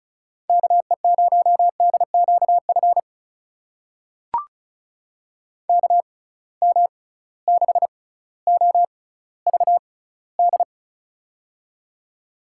Morse code, heard as a keyed tone, about 36 wpm, spaced as for 18 wpm, 700 Hz. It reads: KE0DQF KM6OVD
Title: USA Call Signs - 6 Characters - 6x Character Spacing - 35wpm